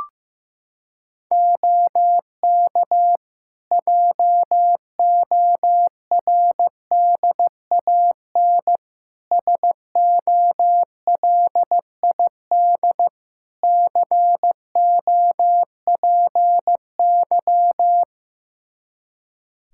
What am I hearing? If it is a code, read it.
Morse code, heard as a keyed tone, about 15 wpm, 700 Hz. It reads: OK JORDAN SOLID COPY
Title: Practice QSO - Exchange 3 - More Info - 15wpm